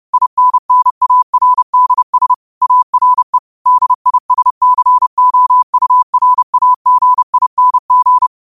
Morse code, heard as a keyed tone, about 30 wpm, 1000 Hz. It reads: INNARDS ARE DISCOURAGING
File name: qotdfiles / 30wpm